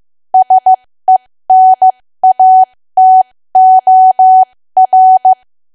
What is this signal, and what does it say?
Morse code, heard as a keyed tone, about 15 wpm, spaced as for 13 wpm, 750 Hz. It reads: SENATOR
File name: cwa / W1AW-Level2 / 13WPM